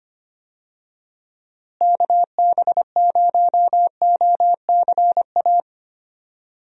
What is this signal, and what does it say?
Morse code, heard as a keyed tone, about 25 wpm, 700 Hz. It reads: KB0OCA
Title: USA Call Signs - 6 Characters - Repeated 2x - 25wpm